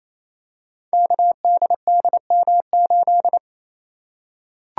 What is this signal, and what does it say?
Morse code, heard as a keyed tone, about 28 wpm, 700 Hz. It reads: KDDM8
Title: Code Groups - 5 Letter:Numbers - 28wpm